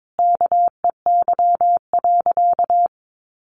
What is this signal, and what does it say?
Morse code, heard as a keyed tone, about 22 wpm, 700 Hz. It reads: KEY.
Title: Sentences from Top 600 Words 22wpm